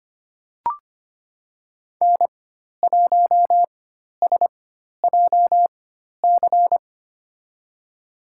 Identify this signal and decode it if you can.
Morse code, heard as a keyed tone, about 25 wpm, spaced as for 12 wpm, 700 Hz. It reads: N1SJC
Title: USA Call Signs - 5 Characters - 4x Character Spacing - 25wpm